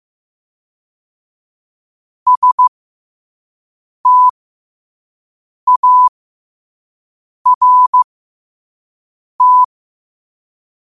Morse code, heard as a keyed tone, about 15 wpm, spaced as for 5 wpm, 1000 Hz. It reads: START
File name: qotdfiles / 5wpm